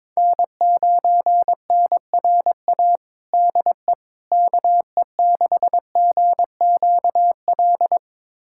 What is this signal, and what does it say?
Morse code, heard as a keyed tone, about 21 wpm, 700 Hz. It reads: N9NRA DE KE6GQL
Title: USA Callsign DE Callsign 22wpm